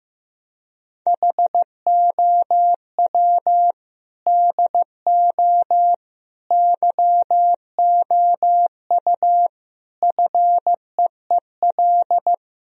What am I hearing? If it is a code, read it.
Morse code, heard as a keyed tone, about 15 wpm, 700 Hz. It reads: HOW DO YOU FEEL
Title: Speed-Racing - Sets of 5 Words 15wpm